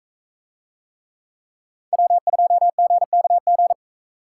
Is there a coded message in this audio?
Yes: Morse code from 1.9 to 3.7 s, heard as a keyed tone, about 42 wpm, 700 Hz. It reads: W2GKG